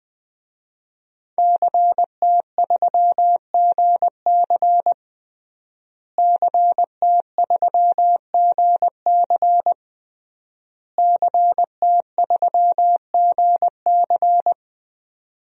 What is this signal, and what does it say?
Morse code, heard as a keyed tone, about 20 wpm, 700 Hz. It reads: CT3GC CT3GC CT3GC
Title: DXCC-100 Call Signs - 5 Characters - Repeated 3x - 20wpm